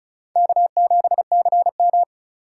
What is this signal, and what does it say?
Morse code, heard as a keyed tone, about 36 wpm, 690 Hz. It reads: K7CM
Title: USA Call Signs - 4 Characters - 2x Character Spacing - 35wpm